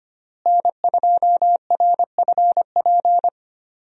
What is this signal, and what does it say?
Morse code, heard as a keyed tone, about 25 wpm, 700 Hz. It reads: N2RFP
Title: USA Call Signs - 5 Characters - Repeated 3x - 25wpm